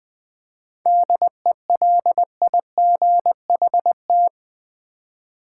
Morse code, heard as a keyed tone, about 20 wpm, 700 Hz. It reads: DELIGHT
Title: 7 Letter Words - First 3 Letters Given - 20wpm